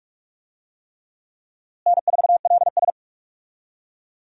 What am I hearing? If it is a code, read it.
Morse code, heard as a keyed tone, about 46 wpm, 700 Hz. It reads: N4LS
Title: USA Call Signs - 4 Characters - Repeated 3x - 45wpm